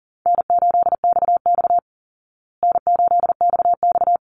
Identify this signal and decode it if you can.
Morse code, heard as a keyed tone, about 40 wpm, 700 Hz. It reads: N8XX N8XX
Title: Call Signs Repeated 2x - 40wpm